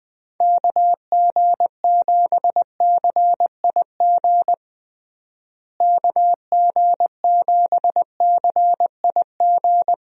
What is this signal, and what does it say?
Morse code, heard as a keyed tone, about 20 wpm, 700 Hz. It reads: KG7CIG KG7CIG
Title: USA Call Signs - 6 Characters - Repeated 2x - 20wpm